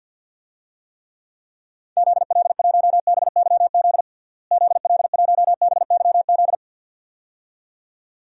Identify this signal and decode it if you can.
Morse code, heard as a keyed tone, about 48 wpm, 700 Hz. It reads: ZL1BYZ ZL1BYZ
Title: DXCC-100 Call Signs - 6 Characters - Repeated 2x - 50wpm